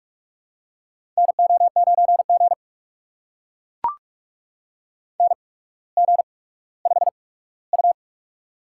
Morse code, heard as a keyed tone, about 44 wpm, spaced as for 19 wpm, 700 Hz. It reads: NO9G NG5U
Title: USA Call Signs - 4 Characters - 8x Character Spacing - 45wpm